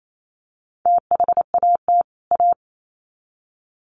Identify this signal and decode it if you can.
Morse code, heard as a keyed tone, about 28 wpm, 700 Hz. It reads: THAT A
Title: Sets of 2 Words 28wpm